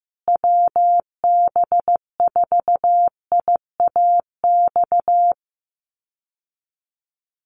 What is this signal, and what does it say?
Morse code, heard as a keyed tone, about 15 wpm, 700 Hz. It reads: WB4IAX
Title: USA Call Signs Repeated 3x 15wpm